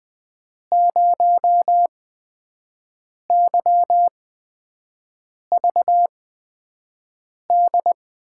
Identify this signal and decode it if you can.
Morse code, heard as a keyed tone, about 20 wpm, spaced as for 5 wpm, 700 Hz. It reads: 0YVD
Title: USA Call Signs - 6 Characters - 8x Character Spacing - 20wpm